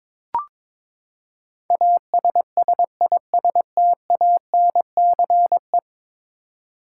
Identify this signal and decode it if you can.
Morse code, heard as a keyed tone, about 22 wpm, 700 Hz. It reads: ASSISTANCE
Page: morse-code-ninja-practice-22wpm